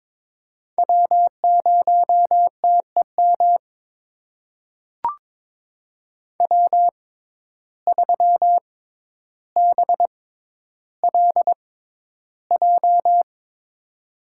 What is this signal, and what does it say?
Morse code, heard as a keyed tone, about 22 wpm, spaced as for 11 wpm, 700 Hz. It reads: W0TEM W3BLJ
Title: USA Call Signs - 5 Characters - 6x Character Spacing - 22wpm